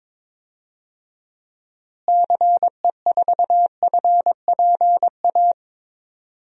Morse code, heard as a keyed tone, about 21 wpm, 700 Hz. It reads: CE4FPA
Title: DXCC-100 Call Signs - 6 Characters - Repeated 2x - 22wpm